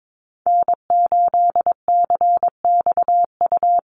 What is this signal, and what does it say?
Morse code, heard as a keyed tone, about 22 wpm, 700 Hz. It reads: N8CXU
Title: USA Call Signs Repeated 2x 22wpm